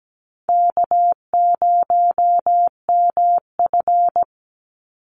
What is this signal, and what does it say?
Morse code, heard as a keyed tone, about 17 wpm, 700 Hz. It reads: K0MF
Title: USA Call Signs 17wpm